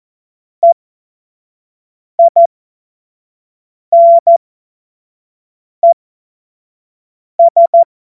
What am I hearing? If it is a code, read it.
Morse code, heard as a keyed tone, about 14 wpm, spaced as for 5 wpm, 680 Hz. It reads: EINES
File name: Morsekurs / Witze / Witz 20. Juni 2023 25 F70